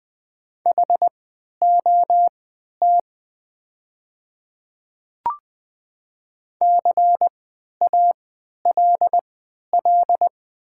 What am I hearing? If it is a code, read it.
Morse code, heard as a keyed tone, about 20 wpm, spaced as for 11 wpm, 700 Hz. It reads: HOT CALL
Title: 3x Character Spacing - Top 100 Words - 20wpm